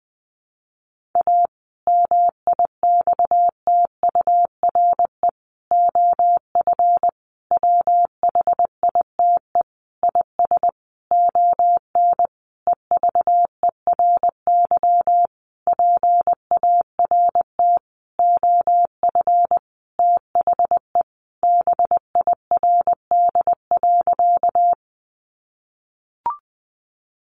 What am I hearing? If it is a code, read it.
Morse code, heard as a keyed tone, about 19 wpm, 700 Hz. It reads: A MIXTURE OF WHITE IS ON EVERY PART OF THE BIRD.
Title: Sentences from Top 1500 Words 20wpm